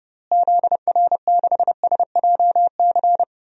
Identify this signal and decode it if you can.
Morse code, heard as a keyed tone, about 32 wpm, 700 Hz. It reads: ZR6SJC
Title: DXCC-100 Call Signs - 6 Characters - Repeated 3x - 30wpm